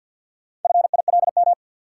Morse code, heard as a keyed tone, about 53 wpm, 700 Hz. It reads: FILM
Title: Four Letter Words - 50wpm